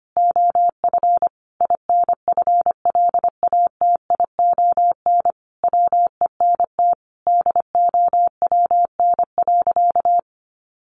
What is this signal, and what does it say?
Morse code, heard as a keyed tone, about 25 wpm, 700 Hz. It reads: OF INFLATION WENT DOWN.
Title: Sentences from Top 1100 Words 25wpm